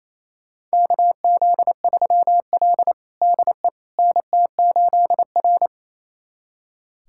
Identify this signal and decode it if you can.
Morse code, heard as a keyed tone, about 28 wpm, 700 Hz. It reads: KZ3L DE NT8R
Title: USA Callsign DE Callsign - 4 Characters - 28wpm